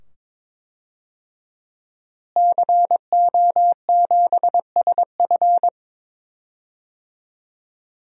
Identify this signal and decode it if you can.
Morse code, heard as a keyed tone, about 22 wpm, 700 Hz. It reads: CO7SF
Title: DXCC-100 Call Signs - 5 Characters - Repeated 2x - 22wpm